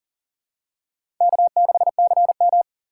Morse code, heard as a keyed tone, about 40 wpm, 700 Hz. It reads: K6CM
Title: USA Call Signs - 4 Characters - 4x Character Spacing - 40wpm